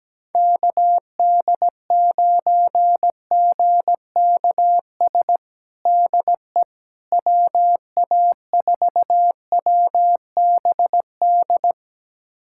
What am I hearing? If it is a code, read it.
Morse code, heard as a keyed tone, about 17 wpm, 700 Hz. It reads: KD9GKS DE WA4WBD
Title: USA Callsign DE Callsign - 6 Characters - 17wpm